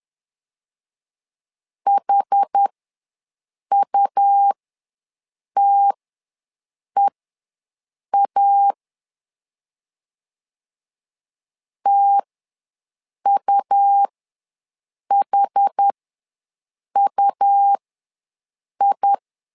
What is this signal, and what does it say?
Morse code, heard as a keyed tone, about 10 wpm, spaced as for 6 wpm, 800 Hz. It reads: HUTEA TUHUI
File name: LEZIONI / 2 UHDN / 2 UH ws 10wpm 2029